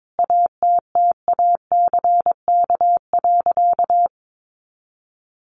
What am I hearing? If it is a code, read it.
Morse code, heard as a keyed tone, about 22 wpm, 700 Hz. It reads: ATTACK.
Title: Sentences from Top 1500 Words 22wpm